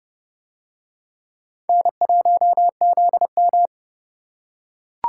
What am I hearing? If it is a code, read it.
Morse code, heard as a keyed tone, about 29 wpm, 700 Hz. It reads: N1ZM